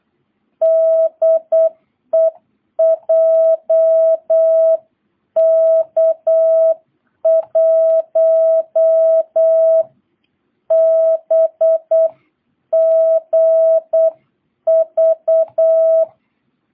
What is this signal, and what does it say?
Morse code, heard as a keyed tone, about 8 wpm, 660 Hz. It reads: DE JK1BGV